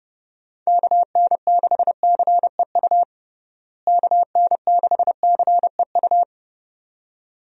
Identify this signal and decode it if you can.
Morse code, heard as a keyed tone, about 32 wpm, 700 Hz. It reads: KN6CEU KN6CEU
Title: USA Call Signs - 6 Characters - Repeated 2x - 30wpm